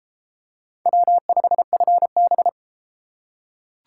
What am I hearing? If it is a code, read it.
Morse code, heard as a keyed tone, about 30 wpm, 700 Hz. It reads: W5FB